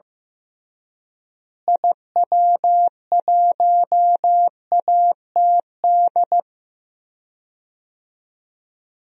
Morse code, heard as a keyed tone, about 15 wpm, 700 Hz. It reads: IW1ATD
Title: DXCC-100 Call Signs - 6 Characters - Repeated 3x - 15wpm